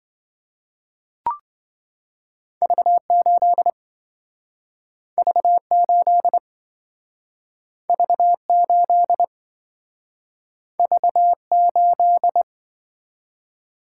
Morse code, heard as a keyed tone, about 27 wpm, 700 Hz. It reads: V8 V8 V8 V8